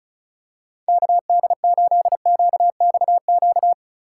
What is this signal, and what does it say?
Morse code, heard as a keyed tone, about 34 wpm, 690 Hz. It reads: KD8QXQ
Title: USA Call Signs - 4x Character Spacing - 35wpm